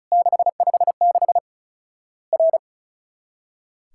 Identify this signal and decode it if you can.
Morse code, heard as a keyed tone, about 36 wpm, 700 Hz. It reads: 656 R
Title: Worked All Europe DX Contest - QTC Exchange - 35wpm